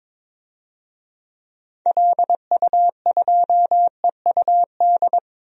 Morse code, heard as a keyed tone, about 22 wpm, 700 Hz. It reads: LU2EUD
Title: DXCC-100 Call Signs - 6 Characters - 22wpm